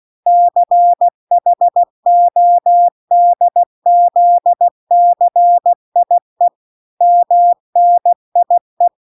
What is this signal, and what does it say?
Morse code, heard as a keyed tone, about 16 wpm, 700 Hz. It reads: CHODZCIE MNIE